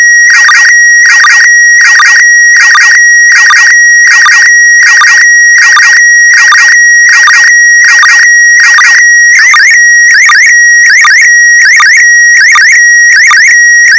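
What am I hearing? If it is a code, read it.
SSTV (slow-scan television), PD 180.
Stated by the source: SSTV test signal in PD180 mode for checking SSTV software without a radio.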